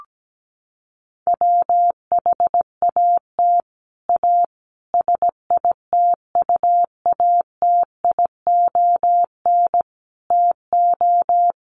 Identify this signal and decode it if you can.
Morse code, heard as a keyed tone, about 17 wpm, 700 Hz. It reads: WHAT A SITUATION TO
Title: Sentences from Top 1200 Words 17wpm